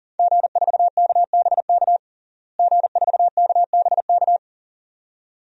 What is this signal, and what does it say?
Morse code, heard as a keyed tone, about 40 wpm, 700 Hz. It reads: G4KBK G4KBK
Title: DXCC-100 Call Signs - 5 Characters - Repeated 2x - 40wpm